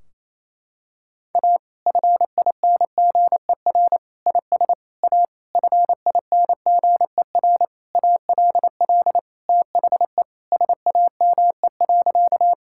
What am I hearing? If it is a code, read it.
Morse code, heard as a keyed tone, about 28 wpm, 700 Hz. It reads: A FINGER IS A FINGER ALL THE SAME.
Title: Sentences from Top 1600 Words 28wpm